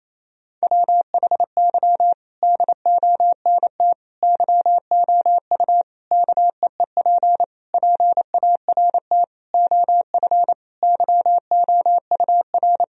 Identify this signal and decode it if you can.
Morse code, heard as a keyed tone, about 28 wpm, 700 Hz. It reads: WHY DONT YOU KEEP PART OF YOUR
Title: Sentences from Top 1000 Words - Review Part 4 - 28wpm